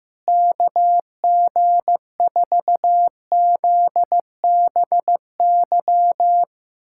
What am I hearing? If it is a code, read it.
Morse code, heard as a keyed tone, about 15 wpm, 700 Hz. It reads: KG4ZBY
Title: USA Callsign DE Callsign 15wpm